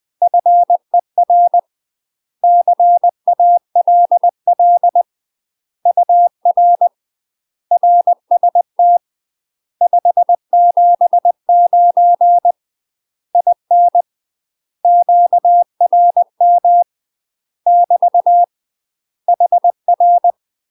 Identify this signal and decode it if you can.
Morse code, heard as a keyed tone, about 20 wpm, 700 Hz. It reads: FER CALL UR RST 579 IN QRM <BT> HR
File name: UK MASTER QSOs TRAINING / 20WPM / UK MASTER QSOs 0036